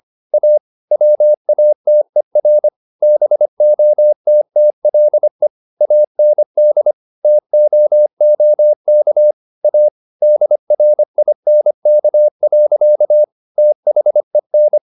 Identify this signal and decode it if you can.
Morse code, heard as a keyed tone, about 25 wpm, 600 Hz. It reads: A WATER BOTTLE AND TOOK A DRINK. THEN